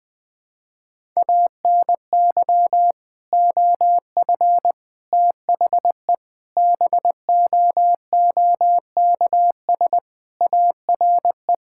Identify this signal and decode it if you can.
Morse code, heard as a keyed tone, about 21 wpm, 700 Hz. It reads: ANY OF THE BOOKS ARE
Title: Sentences with 3-Word Phrases - 20wpm